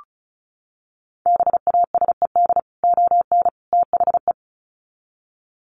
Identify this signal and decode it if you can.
Morse code, heard as a keyed tone, about 34 wpm, 700 Hz. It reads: BASED ON THE
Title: Sets of 3 Words 35wpm